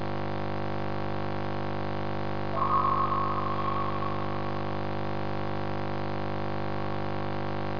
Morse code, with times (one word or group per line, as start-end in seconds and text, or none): none